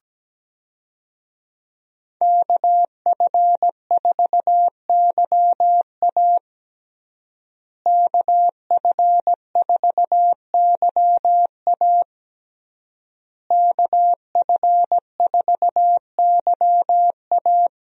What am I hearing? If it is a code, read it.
Morse code, heard as a keyed tone, about 17 wpm, 700 Hz. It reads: KF4YA KF4YA KF4YA
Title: USA Call Signs - 5 Characters - Repeated 3x - 17wpm